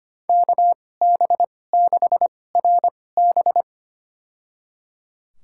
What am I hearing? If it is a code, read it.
Morse code, heard as a keyed tone, about 25 wpm, spaced as for 18 wpm, 700 Hz. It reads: KB6RB
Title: USA Call Signs - 5 Characters - 2x Character Spacing - 25wpm